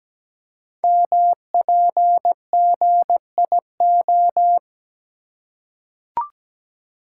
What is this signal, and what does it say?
Morse code, heard as a keyed tone, about 17 wpm, 700 Hz. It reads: MPGIO
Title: Code Groups - 5 Letters - 17wpm